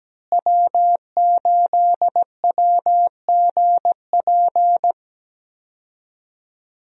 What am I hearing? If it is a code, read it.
Morse code, heard as a keyed tone, about 17 wpm, 700 Hz. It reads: W8WGP
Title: USA Call Signs - 5 Characters - Repeated 3x - 17wpm